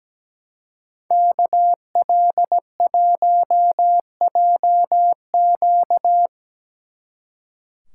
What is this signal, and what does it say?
Morse code, heard as a keyed tone, about 19 wpm, 700 Hz. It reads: KL1JQ
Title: Speed Racing - USA 2x2 Call Signs 17wpm